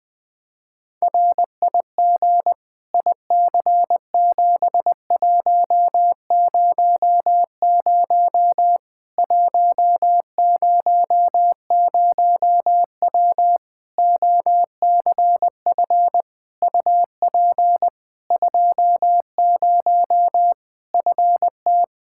Morse code, heard as a keyed tone, about 20 wpm, 700 Hz. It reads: RIG IC7100 100W OCF UP 20 FT
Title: Practice QSO - Exchange 2 - Info - 20wpm